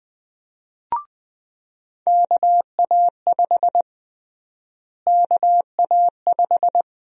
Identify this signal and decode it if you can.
Morse code, heard as a keyed tone, about 20 wpm, 700 Hz. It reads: KA5 KA5
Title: USA Call Sign Prefixes - Repeated 2x - 20wpm